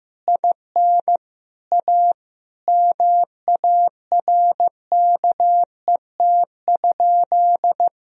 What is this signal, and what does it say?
Morse code, heard as a keyed tone, about 16 wpm, 700 Hz. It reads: IN A MARKET?
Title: Sentences from Top 1000 Words - Review Part 3 - 15wpm